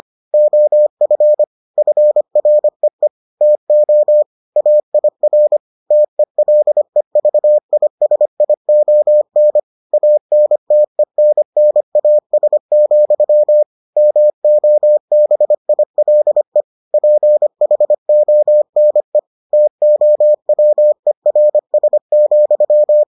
Morse code, heard as a keyed tone, about 25 wpm, 600 Hz. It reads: OF FREE TO AIR TELEVISION ANTENNAS, MOBILE PHONE TOWERS,